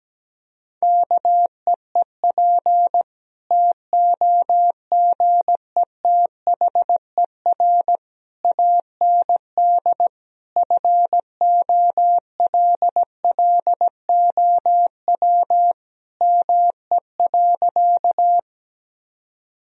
Morse code, heard as a keyed tone, about 17 wpm, 700 Hz. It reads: KEEP TOGETHER AND FOLLOW ME.
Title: Sentences from Top 1000 Words - Review Part 1 - 17wpm